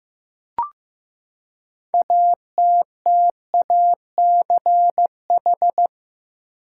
Morse code, heard as a keyed tone, about 15 wpm, 700 Hz. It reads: ATTACH
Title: Six Letter Words - 15wpm